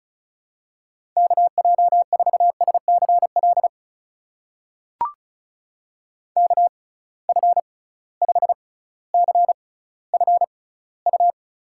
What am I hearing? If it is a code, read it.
Morse code, heard as a keyed tone, about 36 wpm, spaced as for 18 wpm, 700 Hz. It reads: KJ4SCL KF5CFU
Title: USA Call Signs - 6 Characters - 6x Character Spacing - 35wpm